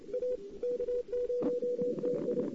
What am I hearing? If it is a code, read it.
Morse code, heard as a keyed tone, about 27 wpm, 500 Hz. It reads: A K0TI
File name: CW / 2008-ss-cw-k5tr-op-n5rz -05:00